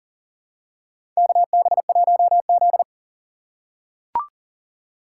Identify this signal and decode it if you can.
Morse code, heard as a keyed tone, about 40 wpm, 700 Hz. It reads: KB1Z
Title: USA Call Signs - 4 Characters - 3x Character Spacing - 40wpm